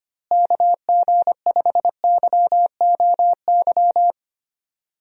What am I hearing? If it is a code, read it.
Morse code, heard as a keyed tone, about 25 wpm, 700 Hz. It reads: KG5YOY
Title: USA Call Signs - 6 Characters - Repeated 3x - 25wpm